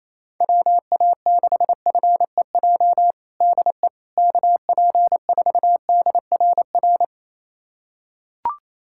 Morse code, heard as a keyed tone, about 27 wpm, 700 Hz. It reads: WA6FEJ DE KP4DRR
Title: USA Callsign DE Callsign - 6 Characters - 28wpm